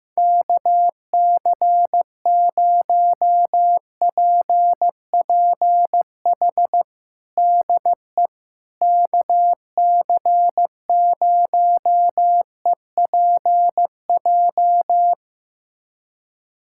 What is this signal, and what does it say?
Morse code, heard as a keyed tone, about 16 wpm, 700 Hz. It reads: KC0PPH DE KC0EPJ
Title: USA Callsign DE Callsign 15wpm